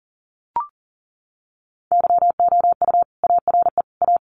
Morse code, heard as a keyed tone, about 40 wpm, 700 Hz. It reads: YOU ARE A
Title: Sentences from Top 700 Words 40wpm